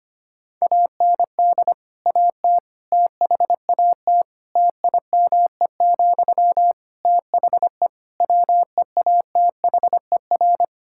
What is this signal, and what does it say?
Morse code, heard as a keyed tone, about 25 wpm, 700 Hz. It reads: AND AT THAT TIME, THE WEATHER
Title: Sentences from Top 1000 Words - Review Part 5 - 25wpm